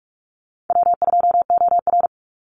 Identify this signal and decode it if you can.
Morse code, heard as a keyed tone, about 44 wpm, 700 Hz. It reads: W2OR
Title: Easy Call Signs Repeated 2x - 45wpm